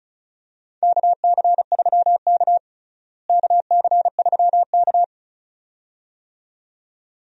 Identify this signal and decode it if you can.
Morse code, heard as a keyed tone, about 36 wpm, 700 Hz. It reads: KC3K KC3K
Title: USA Call Signs - 4 Characters - Repeated 2x - 35wpm